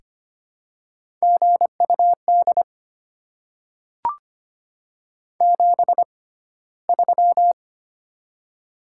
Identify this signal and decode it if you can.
Morse code, heard as a keyed tone, about 25 wpm, spaced as for 15 wpm, 700 Hz. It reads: GUD 73
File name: Top 20 QSO Elements - 6x Character Spacing - 25wpm